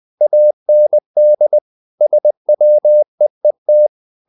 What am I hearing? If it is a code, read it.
Morse code, heard as a keyed tone, about 20 wpm, 600 Hz. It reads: AND SWEET
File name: the old man and the sea part 2 20wpm